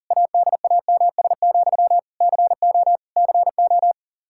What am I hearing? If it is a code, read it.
Morse code, heard as a keyed tone, about 37 wpm, 700 Hz. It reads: ADAMS, CO CO
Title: Practice QSO - Exchange 1 - Beginning - 40wpm